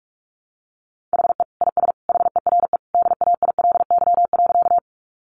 Morse code, heard as a keyed tone, about 44 wpm, 700 Hz. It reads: HE IS HERE DAILY.
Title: Sentences from Top 1100 Words 45wpm